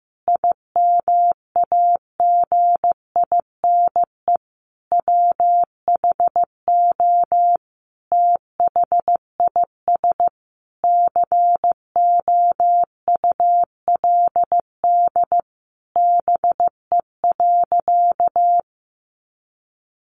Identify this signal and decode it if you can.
Morse code, heard as a keyed tone, about 15 wpm, 700 Hz. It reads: IMAGINE WHO THIS COULD BE.
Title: Sentences from Top 1400 Words 15wpm